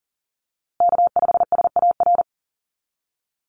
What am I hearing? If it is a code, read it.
Morse code, heard as a keyed tone, about 40 wpm, 700 Hz. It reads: K5SAR
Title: USA Call Signs Repeated 3x 40wpm